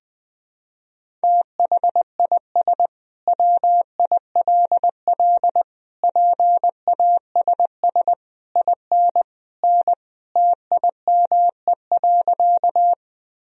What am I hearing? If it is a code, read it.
Morse code, heard as a keyed tone, about 20 wpm, 700 Hz. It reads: THIS WILL PASS IN N TIME.
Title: Sentences from Top 400 Words and Abbreviations 20wpm